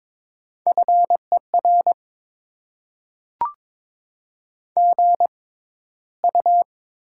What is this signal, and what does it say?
Morse code, heard as a keyed tone, about 22 wpm, spaced as for 13 wpm, 700 Hz. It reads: FER GU
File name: Top 20 QSO Elements - 6x Character Spacing - 22wpm